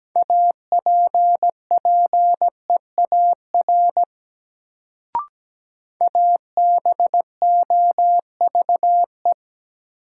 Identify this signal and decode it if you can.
Morse code, heard as a keyed tone, about 17 wpm, 700 Hz. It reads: APPEAR ABOVE
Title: Sentences from Top 1000 Words - Review Part 2 - New Word Drill - 17wpm